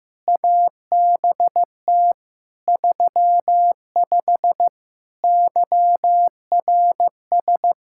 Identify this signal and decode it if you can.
Morse code, heard as a keyed tone, about 15 wpm, 700 Hz. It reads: ABT 35 YRS
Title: Practice QSO - Exchange 3 - More Info - 15wpm